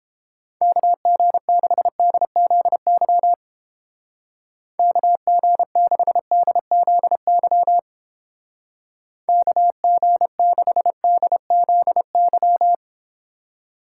Morse code, heard as a keyed tone, about 29 wpm, 700 Hz. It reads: KG6DZY KG6DZY KG6DZY